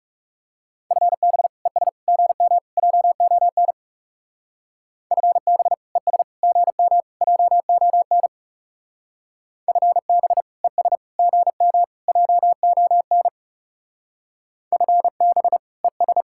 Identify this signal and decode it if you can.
Morse code, heard as a keyed tone, about 35 wpm, 700 Hz. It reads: FB ES GM JON FB ES GM JON FB ES GM JON FB ES
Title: Speed-Racing Common QSO Phrases - 30wpm